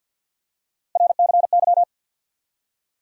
Morse code, heard as a keyed tone, about 49 wpm, 700 Hz. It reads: RCY
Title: USA Call Sign Postfixes - 3 Letters 50wpm